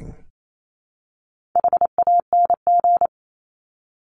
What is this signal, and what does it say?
Morse code, heard as a keyed tone, about 28 wpm, 700 Hz. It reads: HANG
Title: Top 1501-1750 Words 28wpm